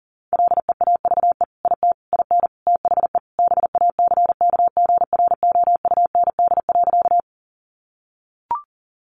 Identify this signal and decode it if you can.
Morse code, heard as a keyed tone, about 40 wpm, 700 Hz. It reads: LEAVE IT IN THE BACKGROUND.
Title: Sentences from Top 1300 Words 40wpm